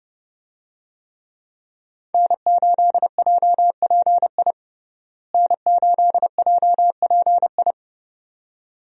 Morse code, heard as a keyed tone, about 30 wpm, 700 Hz. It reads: N8JPI N8JPI
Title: USA Call Signs - 5 Characters - Repeated 2x - 30wpm